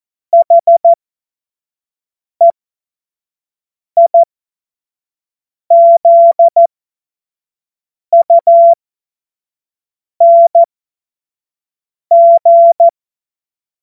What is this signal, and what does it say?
Morse code, heard as a keyed tone, about 14 wpm, spaced as for 5 wpm, 680 Hz. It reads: HEIZUNG
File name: Morsekurs / Witze / Witz 17. Mai 2022 25 F70